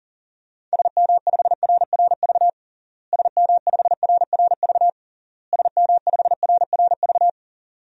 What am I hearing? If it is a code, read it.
Morse code, heard as a keyed tone, about 40 wpm, 700 Hz. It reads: SM5RRV SM5RRV SM5RRV
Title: DXCC-100 Call Signs - 6 Characters - Repeated 3x - 40wpm